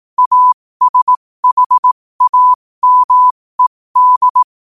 Morse code, heard as a keyed tone, about 18 wpm, spaced as for 15 wpm, 1000 Hz. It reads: ASHAMED
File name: qotdfiles / 15wpm